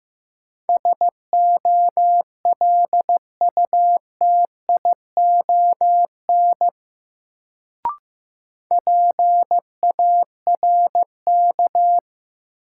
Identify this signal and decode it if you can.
Morse code, heard as a keyed tone, about 15 wpm, 700 Hz. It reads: SOLUTION PARK
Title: Sentences from Top 1300 Words - New Word Drill Alt - 15wpm